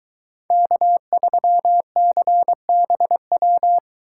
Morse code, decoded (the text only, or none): K3CBW